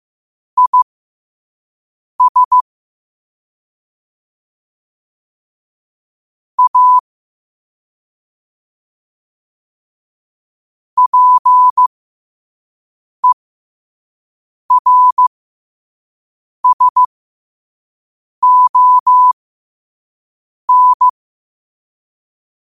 Morse code, heard as a keyed tone, about 15 wpm, spaced as for 5 wpm, 1000 Hz. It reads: IS A PERSON
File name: qotdfiles / 5wpm